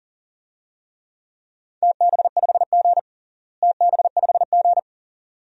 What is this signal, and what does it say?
Morse code, heard as a keyed tone, about 40 wpm, 690 Hz. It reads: TB5G TB5G
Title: DXCC-100 Call Signs - 4 Characters - Repeated 2x - 40wpm